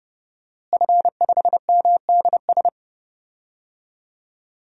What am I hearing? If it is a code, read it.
Morse code, heard as a keyed tone, about 30 wpm, 700 Hz. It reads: F5MDS
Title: DXCC-100 Call Signs - 5 Characters - Repeated 3x - 30wpm